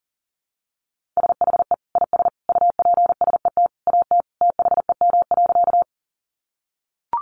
Morse code, heard as a keyed tone, about 40 wpm, 700 Hz. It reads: SHE IS UPSET AT THEM.
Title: Sentences from Top 800 Words 40wpm